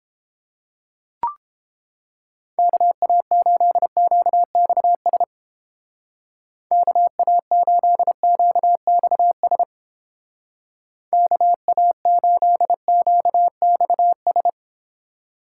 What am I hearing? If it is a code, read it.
Morse code, heard as a keyed tone, about 31 wpm, 700 Hz. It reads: KA8QXS KA8QXS KA8QXS